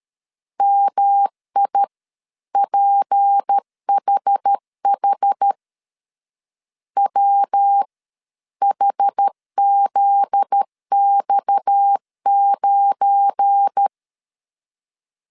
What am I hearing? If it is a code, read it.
Morse code, heard as a keyed tone, about 13 wpm, spaced as for 11 wpm, 800 Hz. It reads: MIPHH WHZX9
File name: CW / DET / 1 9 1500C S B 12wpm 3819